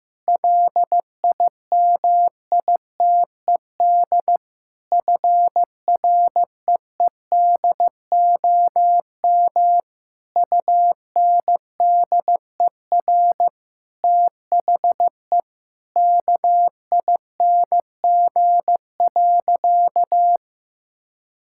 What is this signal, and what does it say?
Morse code, heard as a keyed tone, about 15 wpm, 700 Hz. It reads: LIMITED FREEDOM UNDER THE KING.
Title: Sentences from Top 1000 Words - Review Part 6 - 15wpm